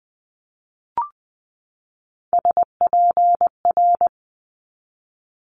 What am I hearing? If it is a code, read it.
Morse code, heard as a keyed tone, about 20 wpm, 700 Hz. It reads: SPR
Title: Multi-letter Phonemes 20wpm